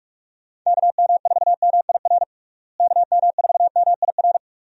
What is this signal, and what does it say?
Morse code, heard as a keyed tone, about 47 wpm, 700 Hz. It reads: KM4MIR KM4MIR
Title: USA Call Signs - 6 Characters - Repeated 2x - 45wpm